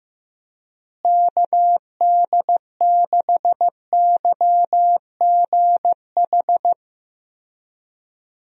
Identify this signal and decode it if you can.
Morse code, heard as a keyed tone, about 15 wpm, 700 Hz. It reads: KD6YGH
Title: USA Call Signs - 6 Characters - Repeated 2x - 15wpm